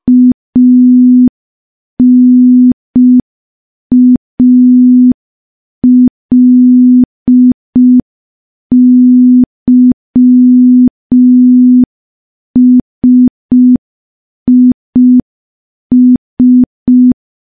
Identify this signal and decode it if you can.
Morse code, heard as a keyed tone, about 5 wpm, 250 Hz. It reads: ANALYSIS
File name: morse / morse-training-05WPM